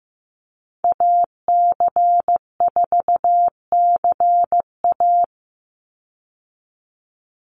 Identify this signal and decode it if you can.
Morse code, heard as a keyed tone, about 15 wpm, 700 Hz. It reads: AC4CA
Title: Call Signs Repeated 3x - 15wpm